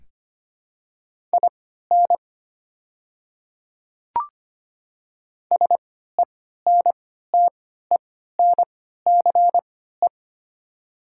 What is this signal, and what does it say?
Morse code, heard as a keyed tone, about 25 wpm, spaced as for 14 wpm, 700 Hz. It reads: IN SENTENCE
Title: 3x Character Spacing - Top 200 Words - 25wpm